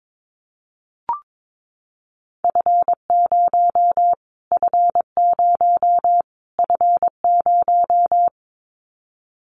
Morse code, heard as a keyed tone, about 22 wpm, 700 Hz. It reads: F0 F0 F0
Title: Callsign Prefixes for Top 25 DXCC - Repeated 3x - 22wpm